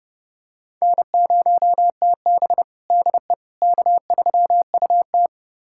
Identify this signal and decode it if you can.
Morse code, heard as a keyed tone, about 30 wpm, 700 Hz. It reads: N0TB DE K3UT
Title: USA Callsign DE Callsign - 4 Characters - 30wpm